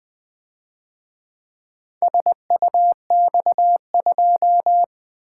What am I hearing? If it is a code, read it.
Morse code, heard as a keyed tone, about 24 wpm, 700 Hz. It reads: SUX2
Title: Speed-Racing - Four Letter-Number Combinations - 20wpm